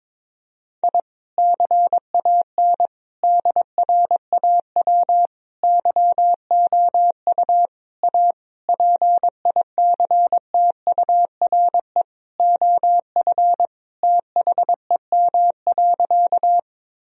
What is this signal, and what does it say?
Morse code, heard as a keyed tone, about 22 wpm, 700 Hz. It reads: I CAN DRAW YOU A PICTURE OF THEM.
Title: Sentences from Top 1000 Words - Review Part 1 - 22wpm